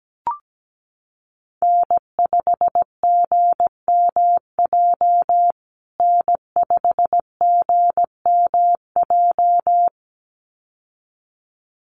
Morse code, heard as a keyed tone, about 17 wpm, 700 Hz. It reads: N5GMJ N5GMJ
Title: USA Call Signs Repeated 2x 17wpm